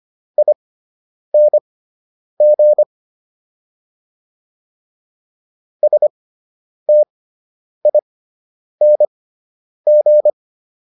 Morse code, heard as a keyed tone, about 25 wpm, spaced as for 9 wpm, 600 Hz. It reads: ING STING